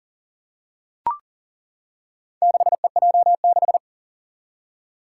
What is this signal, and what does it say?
Morse code, heard as a keyed tone, about 40 wpm, 700 Hz. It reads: 6EJ6
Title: Four Challenging Letter-Number Combinations 40wpm